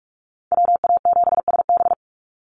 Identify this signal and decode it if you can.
Morse code, heard as a keyed tone, about 44 wpm, 690 Hz. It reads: WA7SB
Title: NA SOTA Chaser Callsigns 45wpm